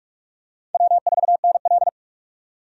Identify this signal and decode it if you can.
Morse code, heard as a keyed tone, about 46 wpm, 700 Hz. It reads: W4NL
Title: USA Call Signs - 4 Characters - 45wpm